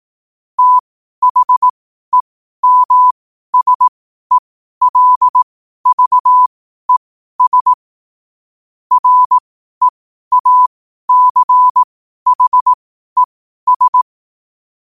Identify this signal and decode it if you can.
Morse code, heard as a keyed tone, about 18 wpm, spaced as for 12 wpm, 1000 Hz. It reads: THEMSELVES REACHES
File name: qotdfiles / 12wpm